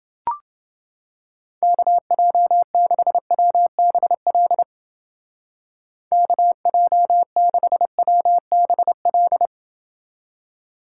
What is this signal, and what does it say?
Morse code, heard as a keyed tone, about 28 wpm, 700 Hz. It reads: KJ6WBL KJ6WBL